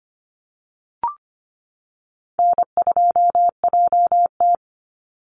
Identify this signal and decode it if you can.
Morse code, heard as a keyed tone, about 25 wpm, 700 Hz. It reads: N2JT
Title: Easy Call Signs - 25wpm